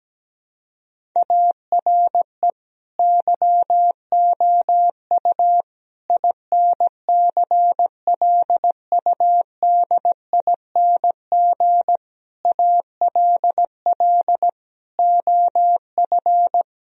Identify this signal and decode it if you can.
Morse code, heard as a keyed tone, about 17 wpm, 700 Hz. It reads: ARE YOU INCLUDING ALL OF
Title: Sentences from Top 1000 Words - Review Part 4 - 17wpm